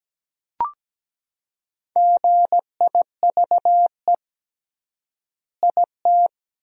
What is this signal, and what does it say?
Morse code, heard as a keyed tone, about 17 wpm, 700 Hz. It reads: GIVE IT
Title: 3x Word-Spacing - Sentences from Top 200 Words - 17wpm